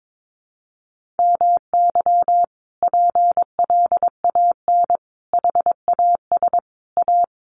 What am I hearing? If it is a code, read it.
Morse code, heard as a keyed tone, about 22 wpm, 700 Hz. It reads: MY PLAN HAS A
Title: Sentences from Top 1300 Words 22wpm